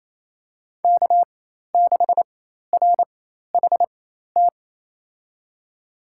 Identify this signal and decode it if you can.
Morse code, heard as a keyed tone, about 28 wpm, spaced as for 13 wpm, 700 Hz. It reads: K6RHT
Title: USA Call Signs - 5 Characters - 4x Character Spacing - 28wpm